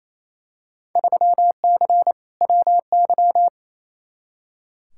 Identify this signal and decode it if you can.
Morse code, heard as a keyed tone, about 28 wpm, 700 Hz. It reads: 3C WY
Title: Field Day Exchanges Encore - 28wpm